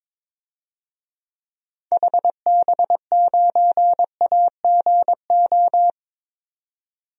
Morse code, heard as a keyed tone, about 22 wpm, 700 Hz. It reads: HB9AGO
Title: DXCC-100 Call Signs - 6 Characters - 22wpm